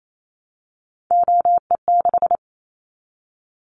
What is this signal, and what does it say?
Morse code, heard as a keyed tone, about 28 wpm, 700 Hz. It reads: OE6
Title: Callsign Prefixes for Top 25 DXCC - 28wpm